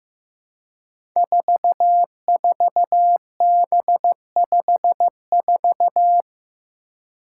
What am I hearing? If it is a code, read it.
Morse code, heard as a keyed tone, about 15 wpm, 700 Hz. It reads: 44B54
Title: Five Challenging Letter-Number Combinations 15wpm